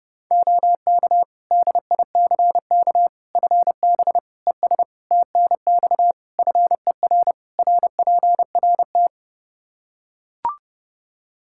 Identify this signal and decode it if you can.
Morse code, heard as a keyed tone, about 32 wpm, 700 Hz. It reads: OK DICK FB ES TNX FER RPRT
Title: Practice QSO - Exchange 2 - Info - 30wpm